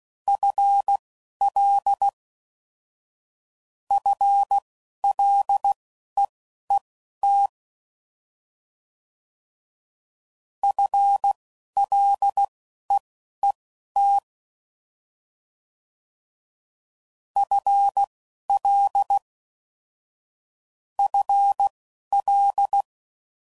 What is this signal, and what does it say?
Morse code, heard as a keyed tone, about 16 wpm, spaced as for 11 wpm, 800 Hz. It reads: FL FLEET FLEET FL FL